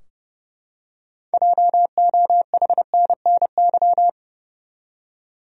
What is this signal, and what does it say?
Morse code, heard as a keyed tone, about 30 wpm, 700 Hz. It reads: JOHNNY
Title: USA Common Names - 30wpm